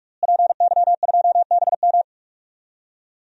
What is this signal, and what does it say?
Morse code, heard as a keyed tone, about 46 wpm, 700 Hz. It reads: PY2BM
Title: DXCC-100 Call Signs - 5 Characters - Repeated 3x - 45wpm